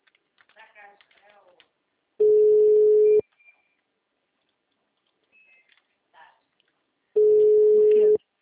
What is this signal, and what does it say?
Ringback tone (Europe, 425 Hz), with speech.
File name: out- -2016- - - .5001